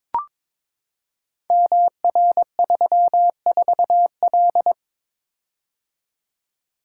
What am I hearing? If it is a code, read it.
Morse code, heard as a keyed tone, about 22 wpm, 700 Hz. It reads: MR34L
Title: Code Groups - 5 Letter:Numbers - 22wpm